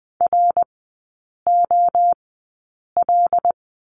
Morse code, heard as a keyed tone, about 20 wpm, spaced as for 8 wpm, 700 Hz. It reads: ROL
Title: 074 Lesson 13e - Letter L - Three Characters 20wpm